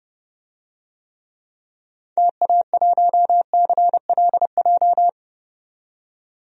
Morse code, heard as a keyed tone, about 31 wpm, 700 Hz. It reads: TA1CLJ
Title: DXCC-100 Call Signs - 6 Characters - 30wpm